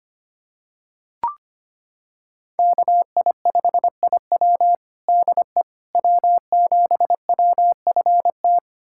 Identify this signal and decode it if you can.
Morse code, heard as a keyed tone, about 25 wpm, 700 Hz. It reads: KI5IW DE W7WFT
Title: USA Callsign DE Callsign 25wpm